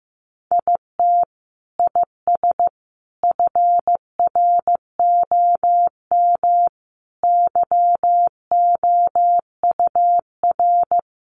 Morse code, heard as a keyed tone, about 15 wpm, 700 Hz. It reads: IT IS FROM YOUR
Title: Sentences from Top 300 Words 15wpm